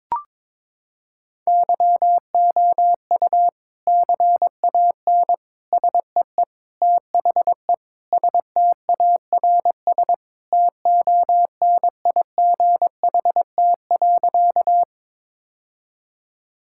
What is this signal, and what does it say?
Morse code, heard as a keyed tone, about 22 wpm, 700 Hz. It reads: YOU CAN SEE THE STARS TONIGHT.
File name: practice / 22wpm / Sentences with 3-Word Phrases - 22wpm